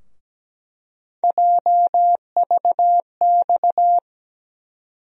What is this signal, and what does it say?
Morse code, heard as a keyed tone, about 17 wpm, 700 Hz. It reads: JVX
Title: USA Call Sign Postfixes - 3 Letters 17wpm